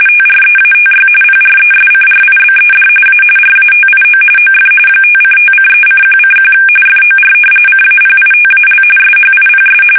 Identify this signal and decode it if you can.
RTTY, about 45.45 baud. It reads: TREATY PRESCRIBED A RATIO OF 5:5:3:1:1 FOR CAPITAL SHIPS
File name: fsk850